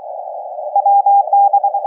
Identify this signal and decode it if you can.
Morse code, heard as a keyed tone, about 22 wpm, 800 Hz. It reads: WD
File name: kf6mwa / cqww cw3